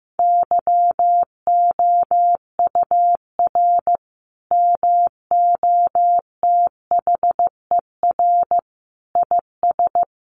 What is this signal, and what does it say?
Morse code, heard as a keyed tone, about 14 wpm, 700 Hz. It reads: YOUR MOTHER IS
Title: Sentences from Top 1100 Words 15wpm